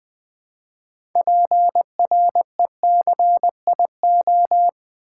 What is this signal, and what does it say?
Morse code, heard as a keyed tone, about 20 wpm, 700 Hz. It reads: PRECIO
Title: Build A Word - 2 Letters At A Time - 8-Letter Words - 20wpm